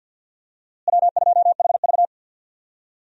Morse code, heard as a keyed tone, about 52 wpm, 700 Hz. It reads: W2HV
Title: USA Call Signs - 4 Characters - Repeated 3x - 50wpm